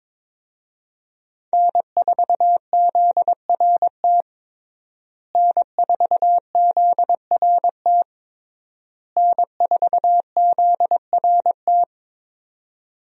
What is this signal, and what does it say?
Morse code, heard as a keyed tone, about 21 wpm, 700 Hz. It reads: N4ZRT N4ZRT N4ZRT
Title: USA Call Signs - 5 Characters - Repeated 3x - 22wpm